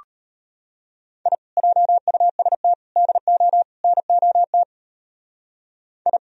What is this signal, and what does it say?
Morse code, heard as a keyed tone, about 31 wpm, 700 Hz. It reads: I JUST DO NOT I
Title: Speed-Racing - Sets of 4 Words 25wpm